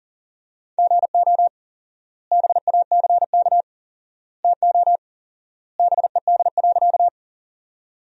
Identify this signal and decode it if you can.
Morse code, heard as a keyed tone, about 40 wpm, 700 Hz. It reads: GO BACK TO BED.
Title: 4x Word-Spacing - Sentences from Top 500 Words - 40wpm